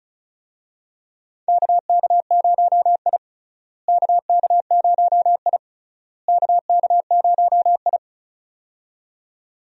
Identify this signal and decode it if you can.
Morse code, heard as a keyed tone, about 36 wpm, 700 Hz. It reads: KK0I KK0I KK0I
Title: USA Call Signs - 4 Characters - Repeated 3x - 35wpm